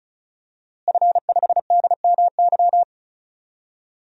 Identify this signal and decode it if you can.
Morse code, heard as a keyed tone, about 36 wpm, 700 Hz. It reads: F5DMY
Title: DXCC-100 Call Signs - 5 Characters - Repeated 2x - 35wpm